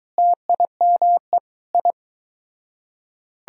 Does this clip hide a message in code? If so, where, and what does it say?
Yes: Morse code from 0.2 to 1.9 s, heard as a keyed tone, about 22 wpm, 700 Hz. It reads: TIME I